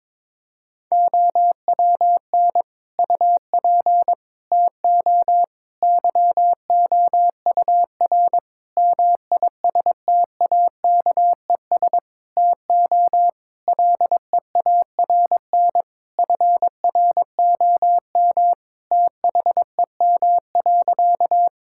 Morse code, heard as a keyed tone, about 21 wpm, 700 Hz. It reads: OWN UP TO YOUR MISTAKES TO LEARN FROM THEM.
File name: Difficult Sentences with Hints - 22wpm